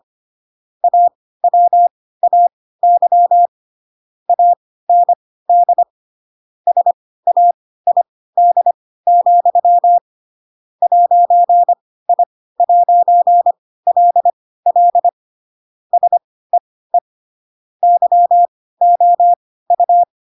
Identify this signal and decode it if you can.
Morse code, heard as a keyed tone, about 25 wpm, spaced as for 16 wpm, 700 Hz. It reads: AWAY AND SAID, 'I'LL SEE YOU
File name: Pinocchio / Pinocchio ch7 10 25 ik1hge